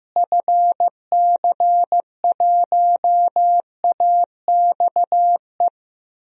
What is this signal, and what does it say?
Morse code, heard as a keyed tone, about 15 wpm, 700 Hz. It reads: FC1AXE
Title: DXCC-100 Call Signs - 6 Characters - Repeated 3x - 15wpm